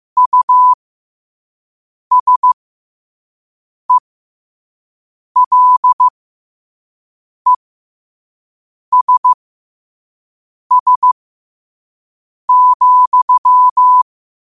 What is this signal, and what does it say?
Morse code, heard as a keyed tone, about 15 wpm, spaced as for 5 wpm, 1000 Hz. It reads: USELESS,
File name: qotdfiles / 5wpm